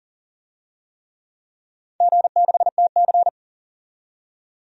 Morse code, heard as a keyed tone, about 44 wpm, 690 Hz. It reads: G6TC
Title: DXCC-100 Call Signs - 4 Characters - 40wpm